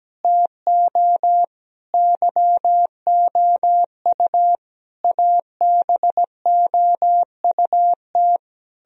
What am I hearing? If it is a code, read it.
Morse code, heard as a keyed tone, about 17 wpm, 700 Hz. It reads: TO YOU ABOUT
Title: Sentences from Top 1000 Words - Review Part 1 - 17wpm